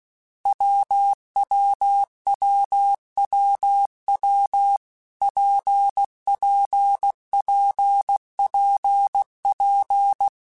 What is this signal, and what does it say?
Morse code, heard as a keyed tone, about 16 wpm, 800 Hz. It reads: WWWWW PPPPP
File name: Difficult Combos / Difficult Combos-mp3s / WPJ WPJ1 @15